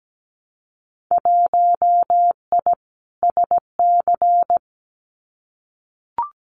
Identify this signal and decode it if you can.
Morse code, heard as a keyed tone, about 17 wpm, 700 Hz. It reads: 1I SC
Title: Winter Field Day Exchanges 17wpm